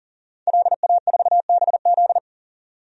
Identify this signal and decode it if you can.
Morse code, heard as a keyed tone, about 37 wpm, 700 Hz. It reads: LA4BZ
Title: DXCC-100 Call Signs - 5 Characters - Repeated 2x - 40wpm